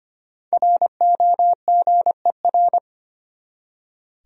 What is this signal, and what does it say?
Morse code, heard as a keyed tone, about 25 wpm, 700 Hz. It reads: ROGER
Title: USA Common Names - Max 5 Letters - 25wpm